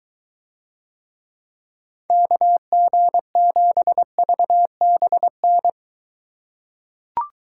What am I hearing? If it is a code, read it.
Morse code, heard as a keyed tone, about 23 wpm, 700 Hz. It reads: KG7VBN